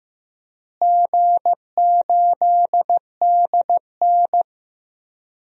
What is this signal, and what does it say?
Morse code, heard as a keyed tone, about 15 wpm, 700 Hz. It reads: G8DN
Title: DXCC-100 Call Signs - 4 Characters - Repeated 3x - 15wpm